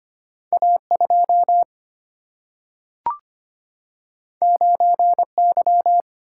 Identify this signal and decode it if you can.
Morse code, heard as a keyed tone, about 25 wpm, 700 Hz. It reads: A 2 9 Y
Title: Single-Letter Single-Number 25wpm